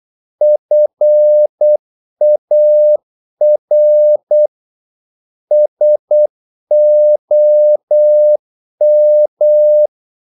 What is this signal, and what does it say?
Morse code, heard as a keyed tone, about 8 wpm, 600 Hz. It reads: FAR SOM
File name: Korte ord DK 8wpm